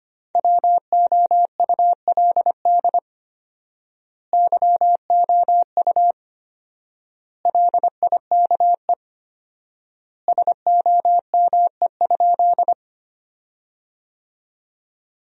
Morse code, heard as a keyed tone, about 25 wpm, 700 Hz. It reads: WOULD YOU LIKE SOME?
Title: 4x Word-Spacing - Sentences from Top 100 Words - 25wpm